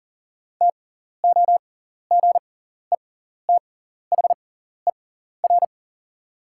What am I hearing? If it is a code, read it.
Morse code, heard as a keyed tone, about 40 wpm, spaced as for 14 wpm, 700 Hz. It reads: TOGETHER
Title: 6x Character Spacing - Top 300 Words - 40wpm